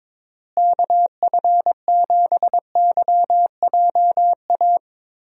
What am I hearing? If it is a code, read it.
Morse code, heard as a keyed tone, about 22 wpm, 700 Hz. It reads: KF7YJA
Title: USA Call Signs - 6 Characters - Repeated 3x - 22wpm